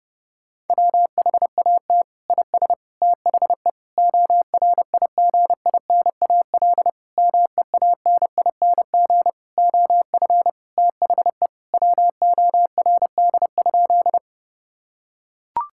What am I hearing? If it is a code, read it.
Morse code, heard as a keyed tone, about 30 wpm, 700 Hz. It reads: WHAT IS THE ORIGINAL MEANING OF THE WORD?
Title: Sentences from Top 1000 Words - Review Part 6 - 30wpm